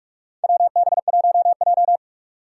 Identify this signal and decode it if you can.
Morse code, heard as a keyed tone, about 46 wpm, 700 Hz. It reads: WB1J
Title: USA Call Signs - 4 Characters - 6x Character Spacing - 45wpm